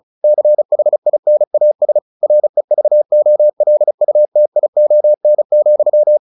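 Morse code, heard as a keyed tone, about 34 wpm, 600 Hz. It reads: CHINAS REVOLUTION,
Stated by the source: Morse practice at 35 wpm from CW med Gnister, sending Ken Follett's novel Never in English.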